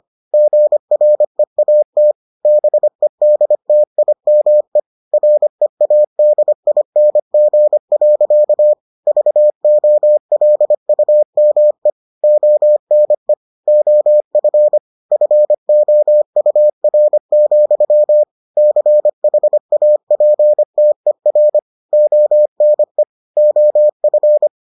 Morse code, heard as a keyed tone, about 25 wpm, 600 Hz. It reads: GREAT BEDTIME READING. VOLUME ONE OF FOUR, CHAPTER ONE OF